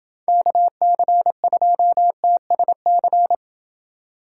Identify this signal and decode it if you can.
Morse code, heard as a keyed tone, about 29 wpm, 700 Hz. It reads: KC2TSC